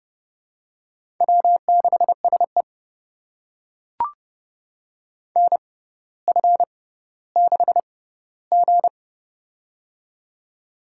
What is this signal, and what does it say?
Morse code, heard as a keyed tone, about 30 wpm, spaced as for 15 wpm, 690 Hz. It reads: W6SE NF6G
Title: USA Call Signs - 4 Characters - 6x Character Spacing - 30wpm